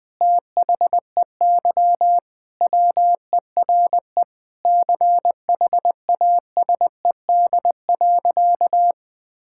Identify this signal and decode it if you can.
Morse code, heard as a keyed tone, about 20 wpm, 700 Hz. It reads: THEY WERE CHASED.
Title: Sentences with 4-Word Phrases - Spoken Beginning - 20wpm